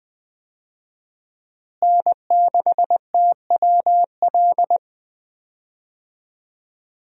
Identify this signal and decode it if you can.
Morse code, heard as a keyed tone, about 20 wpm, 700 Hz. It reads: N6TWL
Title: USA Call Signs - 5 Characters - Repeated 3x - 20wpm